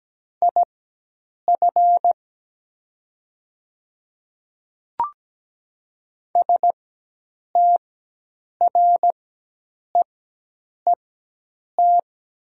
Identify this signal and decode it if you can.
Morse code, heard as a keyed tone, about 17 wpm, spaced as for 8 wpm, 700 Hz. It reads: IF STREET
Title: 4x Character Spacing - Top 500 Words - 17wpm